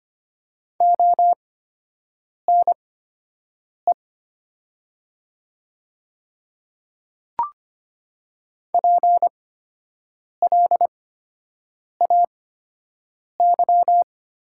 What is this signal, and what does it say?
Morse code, heard as a keyed tone, about 25 wpm, spaced as for 7 wpm, 700 Hz. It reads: ONE PLAY
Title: 8x Character Spacing - Top 300 Words - 25wpm